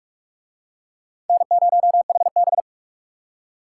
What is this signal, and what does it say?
Morse code, heard as a keyed tone, about 41 wpm, 700 Hz. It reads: N0HB